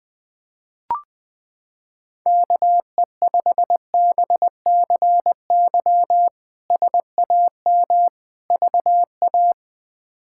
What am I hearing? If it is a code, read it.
Morse code, heard as a keyed tone, about 20 wpm, 700 Hz. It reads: KE5BCY SAM VA
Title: K1USN SST QSO Practice - 20wpm